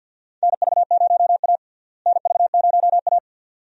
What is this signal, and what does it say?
Morse code, heard as a keyed tone, about 49 wpm, 700 Hz. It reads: NV0A NV0A
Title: USA Call Signs - 4 Characters - Repeated 2x - 50wpm